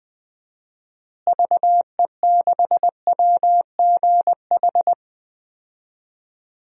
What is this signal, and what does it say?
Morse code, heard as a keyed tone, about 20 wpm, 700 Hz. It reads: VE6WGH
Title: DXCC-100 Call Signs - 6 Characters - Repeated 2x - 20wpm